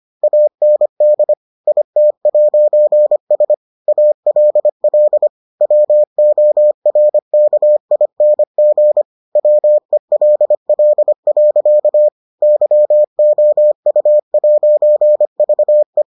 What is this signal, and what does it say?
Morse code, heard as a keyed tone, about 25 wpm, 600 Hz. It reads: AND IT'S ALL WORKING WELL. YOU'VE